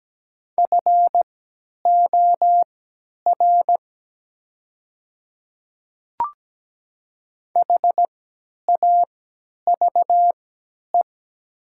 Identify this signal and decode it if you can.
Morse code, heard as a keyed tone, about 17 wpm, spaced as for 10 wpm, 700 Hz. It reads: FOR HAVE
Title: 3x Character Spacing - Top 200 Words - 17wpm